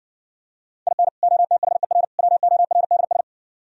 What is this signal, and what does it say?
Morse code, heard as a keyed tone, about 59 wpm, 700 Hz. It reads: IN OTHER WORDS